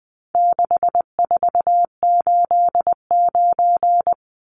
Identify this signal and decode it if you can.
Morse code, heard as a keyed tone, about 20 wpm, 700 Hz. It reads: 6489
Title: Numbers - 4-Digit 20wpm